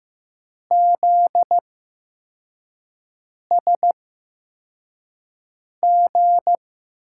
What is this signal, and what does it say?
Morse code, heard as a keyed tone, about 15 wpm, spaced as for 4 wpm, 700 Hz. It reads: ZSG
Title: USA Call Signs - 6 Characters - 8x Character Spacing - 15wpm